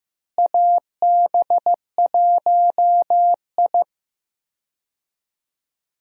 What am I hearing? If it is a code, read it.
Morse code, heard as a keyed tone, about 15 wpm, 700 Hz. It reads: AB1I
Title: USA Call Signs - 4 Characters - Repeated 3x - 15wpm